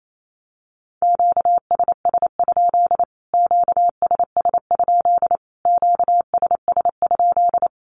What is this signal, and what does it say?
Morse code, heard as a keyed tone, about 28 wpm, 700 Hz. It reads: QSS? QSS? QSS?
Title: Q-Codes Repeated 3x 28wpm